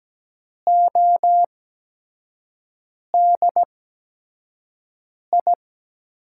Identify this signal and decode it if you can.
Morse code, heard as a keyed tone, about 17 wpm, spaced as for 5 wpm, 700 Hz. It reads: ODI
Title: USA Call Signs - 5 Characters - 8x Character Spacing - 17wpm